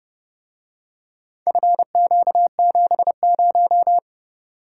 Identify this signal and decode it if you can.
Morse code, heard as a keyed tone, about 31 wpm, 700 Hz. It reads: FQ70